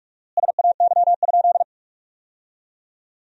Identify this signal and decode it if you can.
Morse code, heard as a keyed tone, about 47 wpm, 700 Hz. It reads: SAY?
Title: 4x Word-Spacing - Sentences from Top 500 Words - 45wpm